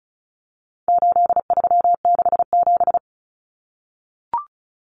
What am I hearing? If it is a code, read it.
Morse code, heard as a keyed tone, about 34 wpm, 700 Hz. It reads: 8367
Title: Common Cut Numbers - 4-Digit 35wpm